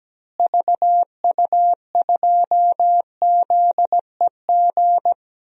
Morse code, heard as a keyed tone, about 17 wpm, 700 Hz. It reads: VU2ZEG
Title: DXCC-100 Call Signs - 6 Characters - Repeated 3x - 17wpm